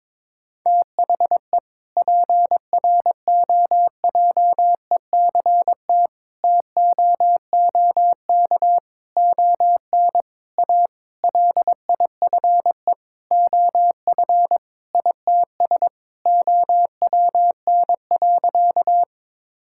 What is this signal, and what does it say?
Morse code, heard as a keyed tone, about 22 wpm, 700 Hz. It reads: THE PROJECT TOOK ON A LIFE OF ITS OWN.
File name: Sentences with 7-Word Phrases - 22wpm